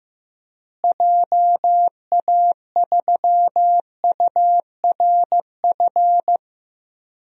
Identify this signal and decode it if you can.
Morse code, heard as a keyed tone, about 15 wpm, 700 Hz. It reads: JA3URF
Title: DXCC-100 Call Signs - 6 Characters - Repeated 2x - 15wpm